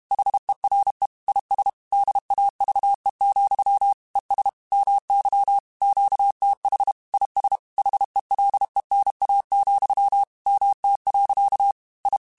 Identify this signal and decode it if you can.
Morse code, heard as a keyed tone, about 32 wpm, 800 Hz. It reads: HERE IS DAVE, ES MY QTH IS HELENA, MT. I